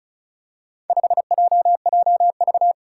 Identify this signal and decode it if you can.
Morse code, heard as a keyed tone, about 36 wpm, 700 Hz. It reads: 5JJV
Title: Four Challenging Letter-Number Combinations 35wpm